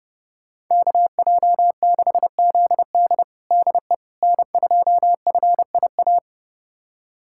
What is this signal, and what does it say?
Morse code, heard as a keyed tone, about 30 wpm, 700 Hz. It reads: KJ6ZD DE N2FIA
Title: USA Callsign DE Callsign - 5 Characters - 30wpm